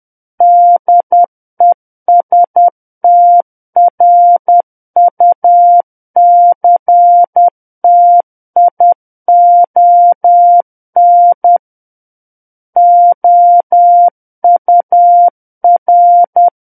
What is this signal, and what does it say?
Morse code, heard as a keyed tone, about 10 wpm, 700 Hz. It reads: DESTRUCTION OUR
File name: Book One / 01-Chapter 1 - 10 WPM